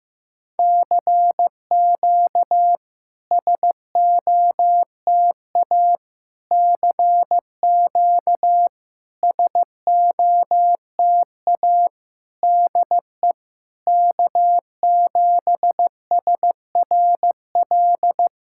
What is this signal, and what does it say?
Morse code, heard as a keyed tone, about 15 wpm, 690 Hz. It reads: CQ SOTA CQ SOTA DE K7SRL
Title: SOTA QSO Practice 15wpm